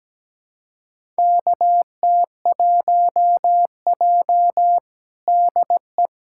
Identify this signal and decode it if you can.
Morse code, heard as a keyed tone, about 17 wpm, 700 Hz. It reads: KT1J DE
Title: USA Callsign DE Callsign - 4 Characters - 17wpm